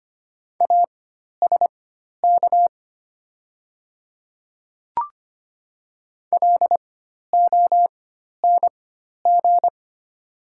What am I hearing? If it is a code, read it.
Morse code, heard as a keyed tone, about 25 wpm, spaced as for 12 wpm, 700 Hz. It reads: ASK LONG
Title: 4x Character Spacing - Top 300 Words - 25wpm